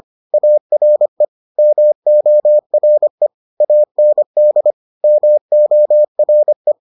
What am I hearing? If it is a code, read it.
Morse code, heard as a keyed tone, about 25 wpm, 600 Hz. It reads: ARE MORE AND MORE